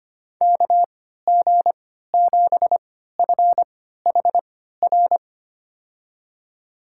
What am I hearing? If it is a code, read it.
Morse code, heard as a keyed tone, about 25 wpm, spaced as for 14 wpm, 700 Hz. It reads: KG7FHR
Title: USA Call Signs - 6 Characters - 3x Character Spacing - 25wpm